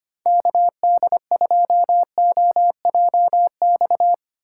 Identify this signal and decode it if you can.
Morse code, heard as a keyed tone, about 25 wpm, 700 Hz. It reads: KD2OJX
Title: USA Call Signs - 4x Character Spacing - 25wpm